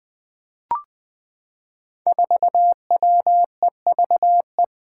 Morse code, heard as a keyed tone, about 21 wpm, 700 Hz. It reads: 4WEVE
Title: Five Challenging Letter-Number Combinations 20wpm